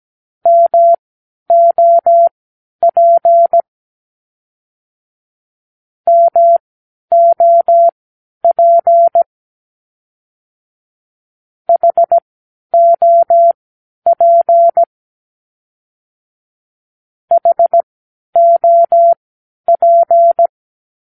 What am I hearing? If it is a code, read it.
Morse code, heard as a keyed tone, about 17 wpm, spaced as for 10 wpm, 680 Hz. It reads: MOP MOP HOP HOP
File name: k7qo / Practicewords